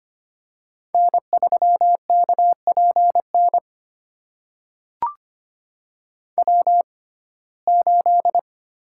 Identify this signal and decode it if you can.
Morse code, heard as a keyed tone, about 25 wpm, spaced as for 18 wpm, 700 Hz. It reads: N3KPN W8
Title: USA Call Signs - 6x Character Spacing - 25wpm